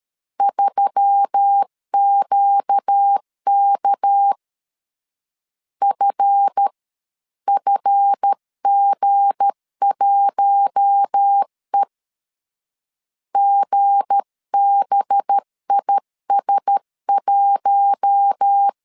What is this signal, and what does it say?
Morse code, heard as a keyed tone, about 13 wpm, spaced as for 10 wpm, 800 Hz. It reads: 3QKFF G1E GBIS1
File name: LEZIONI / 7 GF9Z / 7 GF 12wpm 2122